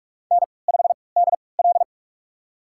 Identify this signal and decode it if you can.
Morse code, heard as a keyed tone, about 47 wpm, spaced as for 26 wpm, 700 Hz. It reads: N5DL
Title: USA Call Signs - 4 Characters - 3x Character Spacing - 45wpm